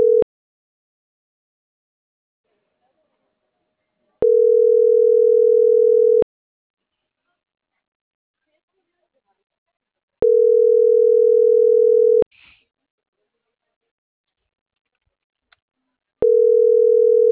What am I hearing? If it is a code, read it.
Ringback tone (North America), 440 + 480 Hz.